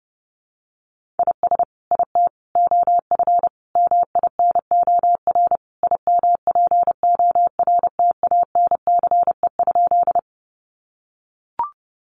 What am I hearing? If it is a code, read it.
Morse code, heard as a keyed tone, about 30 wpm, 700 Hz. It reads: IS IT OF MINOR IMPORTANCE?
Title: Sentences from Top 1500 Words 30wpm